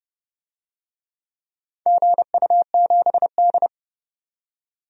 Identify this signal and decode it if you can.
Morse code, heard as a keyed tone, about 29 wpm, 700 Hz. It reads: GU7D
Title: DXCC-100 Call Signs - 4 Characters - 30wpm